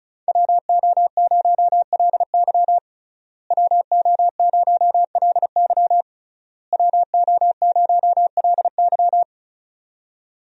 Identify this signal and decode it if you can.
Morse code, heard as a keyed tone, about 34 wpm, 700 Hz. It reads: WO0LY WO0LY WO0LY
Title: USA Call Signs - 5 Characters - Repeated 3x - 35wpm